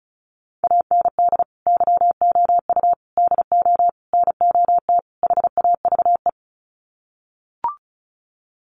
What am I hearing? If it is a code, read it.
Morse code, heard as a keyed tone, about 34 wpm, 700 Hz. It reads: AND YOU DO NOT HAVE
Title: Sets of 5 Words Encore 35wpm